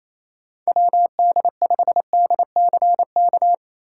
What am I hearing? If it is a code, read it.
Morse code, heard as a keyed tone, about 28 wpm, 700 Hz. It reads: WD5DCK
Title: USA Call Signs - 6 Characters - 3x Character Spacing - 28wpm